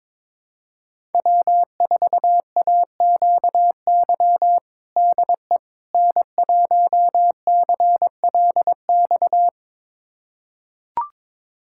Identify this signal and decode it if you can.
Morse code, heard as a keyed tone, about 22 wpm, 700 Hz. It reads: W4AQY DE N1CLX
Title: USA Callsign DE Callsign - 5 Characters - 22wpm